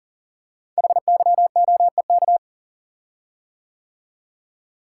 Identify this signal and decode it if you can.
Morse code, heard as a keyed tone, about 39 wpm, 700 Hz. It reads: HYOEK
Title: Code Groups - 5 Letters - 40wpm